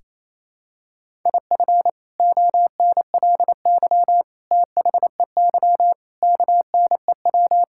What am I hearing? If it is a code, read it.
Morse code, heard as a keyed tone, about 28 wpm, 700 Hz. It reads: IF ONLY THEY KNEW
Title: Sentences from Top 1000 Words - Review Part 6 - 28wpm